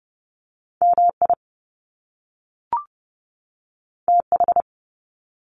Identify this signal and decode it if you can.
Morse code, heard as a keyed tone, about 30 wpm, 700 Hz. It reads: M I T H
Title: Top Two Letter Combinations in Words 30wpm